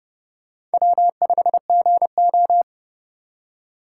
Morse code, heard as a keyed tone, about 28 wpm, 700 Hz. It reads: W5GO